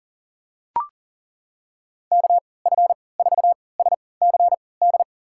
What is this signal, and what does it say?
Morse code, heard as a keyed tone, about 39 wpm, spaced as for 23 wpm, 700 Hz. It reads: KF4SCD
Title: USA Call Signs - 6 Characters - 3x Character Spacing - 40wpm